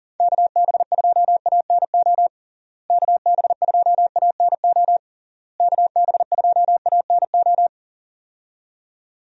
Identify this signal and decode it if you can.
Morse code, heard as a keyed tone, about 40 wpm, 700 Hz. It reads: KB2ANO KB2ANO KB2ANO
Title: USA Call Signs - 6 Characters - Repeated 3x - 40wpm